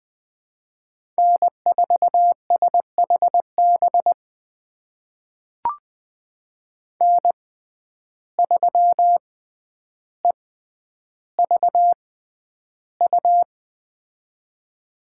Morse code, heard as a keyed tone, about 20 wpm, spaced as for 10 wpm, 700 Hz. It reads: N4SHB N3EVU
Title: USA Call Signs - 5 Characters - 6x Character Spacing - 20wpm